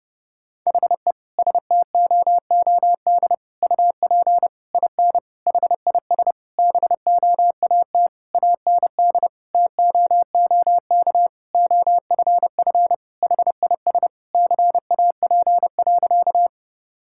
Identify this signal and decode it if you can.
Morse code, heard as a keyed tone, about 32 wpm, 700 Hz. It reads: HE STOOD UP IN HIS BOAT AND TOOK OFF HIS CAP.
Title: Sentences from Top 1600 Words 30wpm